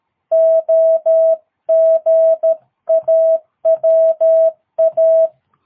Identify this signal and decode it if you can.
Morse code, heard as a keyed tone, about 13 wpm, 660 Hz. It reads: OGAWA
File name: jh8cbh / 2022cw